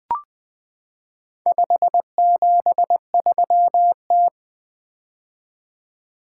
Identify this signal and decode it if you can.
Morse code, heard as a keyed tone, about 20 wpm, 700 Hz. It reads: 573T
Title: Four Letter-Number Combinations - 20wpm